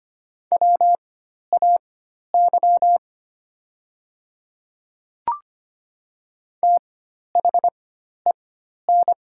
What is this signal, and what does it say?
Morse code, heard as a keyed tone, about 25 wpm, spaced as for 12 wpm, 700 Hz. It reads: WAY THEN
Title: 4x Character Spacing - Top 100 Words - 25wpm